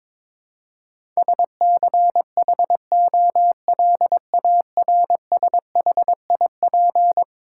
Morse code, heard as a keyed tone, about 22 wpm, 700 Hz. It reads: SCHOLARSHIP
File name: practice / 22wpm / Eleven Letter Words - 22wpm